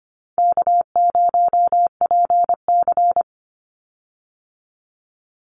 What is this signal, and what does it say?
Morse code, heard as a keyed tone, about 25 wpm, 700 Hz. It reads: K0PC
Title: Easy Call Signs - 25wpm